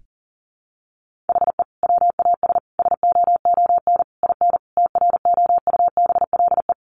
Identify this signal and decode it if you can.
Morse code, heard as a keyed tone, about 40 wpm, 700 Hz. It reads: HE WAS SOON IN TROUBLE
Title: Sentences from Top 1200 Words 40wpm